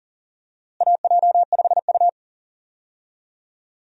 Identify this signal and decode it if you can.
Morse code, heard as a keyed tone, about 40 wpm, 700 Hz. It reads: AJ5U
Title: USA Call Signs - 4 Characters - Repeated 3x - 40wpm